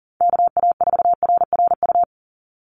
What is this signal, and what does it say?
Morse code, heard as a keyed tone, about 40 wpm, 700 Hz. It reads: KA4RRU
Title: Call Signs Repeated 3x - 40wpm